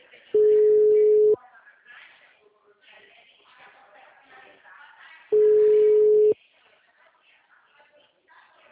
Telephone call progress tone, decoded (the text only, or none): ringback tone